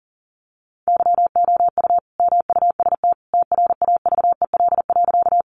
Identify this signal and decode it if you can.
Morse code, heard as a keyed tone, about 40 wpm, 700 Hz. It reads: YOU MUST TRAVEL.
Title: Sentences from Top 400 Words 40wpm